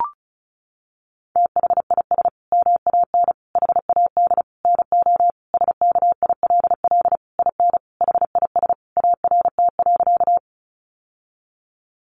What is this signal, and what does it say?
Morse code, heard as a keyed tone, about 34 wpm, 700 Hz. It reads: THIS MAN HAD NO SKILL IN HIS ART.
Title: Sentences from Top 1200 Words 35wpm